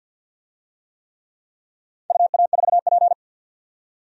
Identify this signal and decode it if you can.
Morse code, heard as a keyed tone, about 52 wpm, 700 Hz. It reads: UA4P
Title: DXCC-100 Call Signs - 4 Characters - 50wpm